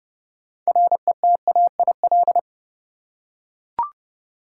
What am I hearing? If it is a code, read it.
Morse code, heard as a keyed tone, about 30 wpm, 700 Hz. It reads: RETAIL
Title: Practice QSO - Exchange 3 - More Info - 30wpm